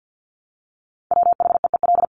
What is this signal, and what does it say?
Morse code, heard as a keyed tone, about 50 wpm, 700 Hz. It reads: WHEEL
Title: Top 500 Words Repeated 50wpm